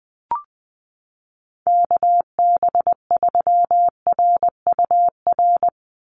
Morse code, heard as a keyed tone, about 20 wpm, 700 Hz. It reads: KB3RUR
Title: USA Call Signs 20wpm